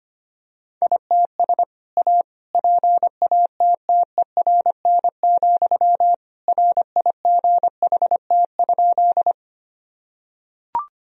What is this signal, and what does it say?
Morse code, heard as a keyed tone, about 25 wpm, 700 Hz. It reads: ITS A PATTERN, RIGHT?
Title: Sentences from Top 1000 Words - Review Part 2 - 25wpm